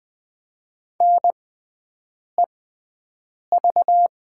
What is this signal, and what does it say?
Morse code, heard as a keyed tone, about 20 wpm, spaced as for 7 wpm, 700 Hz. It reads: NEV
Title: 6x Character Spacing - Top 300 Words - 20wpm